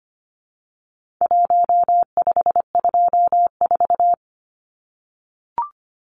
Morse code, heard as a keyed tone, about 25 wpm, 700 Hz. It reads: 1524
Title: Numbers - 4-Digit 25wpm